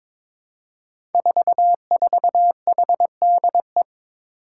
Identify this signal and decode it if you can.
Morse code, heard as a keyed tone, about 22 wpm, 700 Hz. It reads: 44HDE
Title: Five Challenging Letter-Number Combinations 22wpm